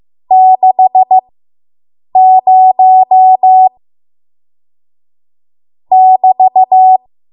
Morse code, heard as a keyed tone, about 15 wpm, spaced as for 7 wpm, 750 Hz. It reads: 60 <BT>